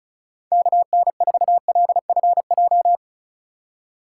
Morse code, heard as a keyed tone, about 34 wpm, 700 Hz. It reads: KN4LFJ
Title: USA Call Signs - 6 Characters - 2x Character Spacing - 35wpm